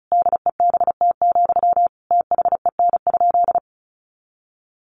Morse code, heard as a keyed tone, about 34 wpm, 700 Hz. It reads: DEBT, THEN?
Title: Sentences from Top 1300 Words 35wpm